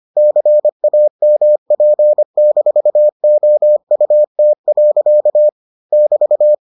Morse code, heard as a keyed tone, about 25 wpm, 600 Hz. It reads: CAMP-OUT. <BT>